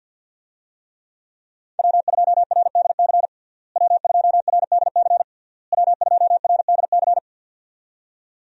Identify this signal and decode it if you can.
Morse code, heard as a keyed tone, about 52 wpm, 700 Hz. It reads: W2RDC W2RDC W2RDC
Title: USA Call Signs - 5 Characters - Repeated 3x - 50wpm